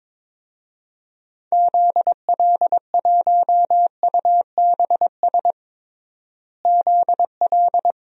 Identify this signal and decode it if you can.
Morse code, heard as a keyed tone, about 21 wpm, 700 Hz. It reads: ZL1UBS ZL
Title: DXCC-100 Call Signs - 6 Characters - Repeated 3x - 22wpm